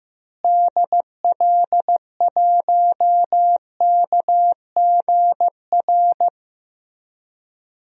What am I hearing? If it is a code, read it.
Morse code, heard as a keyed tone, about 15 wpm, 700 Hz. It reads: DL1KGR
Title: DXCC-100 Call Signs - 6 Characters - Repeated 3x - 15wpm